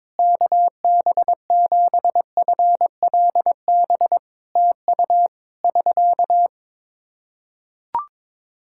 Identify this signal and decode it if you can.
Morse code, heard as a keyed tone, about 22 wpm, 700 Hz. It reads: KB7FLB TU <SK>
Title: Practice QSO - Exchange 4 - Ending - 22wpm